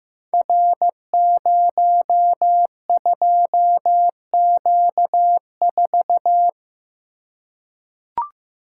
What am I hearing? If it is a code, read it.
Morse code, heard as a keyed tone, about 15 wpm, 700 Hz. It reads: R02Q4
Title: Code Groups - 5 Letter:Numbers - 15wpm